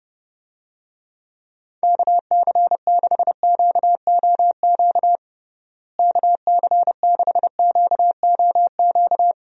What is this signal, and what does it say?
Morse code, heard as a keyed tone, about 30 wpm, 700 Hz. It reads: KC6QOQ KC6QOQ
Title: USA Call Signs - 6 Characters - Repeated 3x - 30wpm